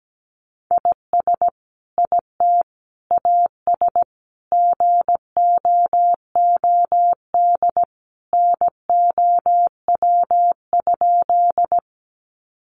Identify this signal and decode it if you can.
Morse code, heard as a keyed tone, about 17 wpm, 700 Hz. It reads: IS IT AS GOOD NOW?
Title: Sentences from Top 200 Words 17wpm